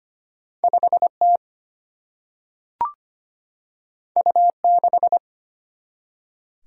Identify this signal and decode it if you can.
Morse code, heard as a keyed tone, about 25 wpm, 700 Hz. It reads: 5 T U 6
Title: Single-Letter Single-Number 25wpm